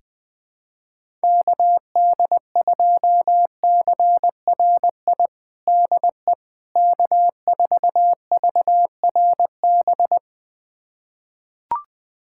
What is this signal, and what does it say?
Morse code, heard as a keyed tone, about 20 wpm, 700 Hz. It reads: KD2CRI DE K4VRB
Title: USA Callsign DE Callsign 20wpm